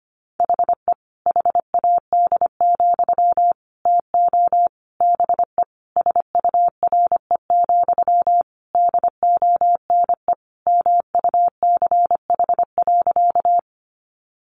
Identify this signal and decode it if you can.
Morse code, heard as a keyed tone, about 25 wpm, 700 Hz. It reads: HE HAD, TO BE SURE, DONE MUCH.
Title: Sentences from Top 500 Words 25wpm